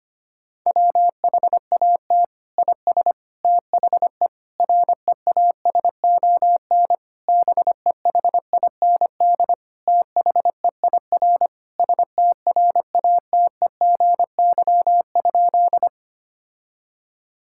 Morse code, heard as a keyed tone, about 25 wpm, 700 Hz. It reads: WHAT IS THE REASON BEHIND THEIR STRATEGY?
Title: Sentences from Top 1000 Words - Review Part 5 - 25wpm